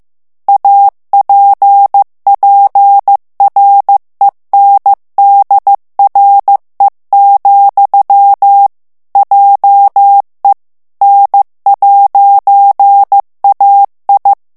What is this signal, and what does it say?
Morse code, heard as a keyed tone, about 15 wpm, 800 Hz. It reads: APPRENDRE, JE N'AI
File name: morse mp3 / FRANCAIS TEXTE CLAIR 2 VITESSE 14 MOTS MINUTES